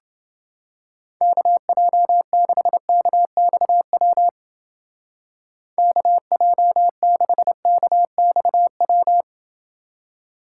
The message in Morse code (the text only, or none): KJ6KXW KJ6KXW